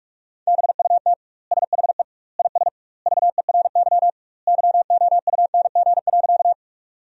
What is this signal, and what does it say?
Morse code, heard as a keyed tone, about 46 wpm, 700 Hz. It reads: BUT SHE IS VERY YOUNG.
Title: 2x Word-Spacing - Sentences from Top 400 Words - 45wpm